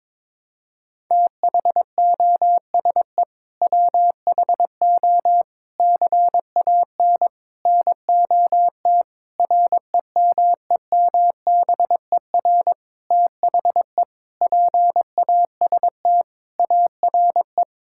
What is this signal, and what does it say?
Morse code, heard as a keyed tone, about 22 wpm, 700 Hz. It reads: THOSE WHO CAN NOT REMEMBER THE PAST ARE
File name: practice / 22wpm / Sentences with 8-Word Phrases - 22wpm